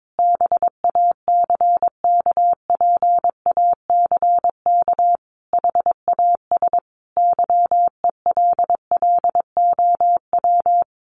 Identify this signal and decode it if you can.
Morse code, heard as a keyed tone, about 22 wpm, 700 Hz. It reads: BACKPACK HAS YELLOW
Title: Additional Easy Phrases 22wpm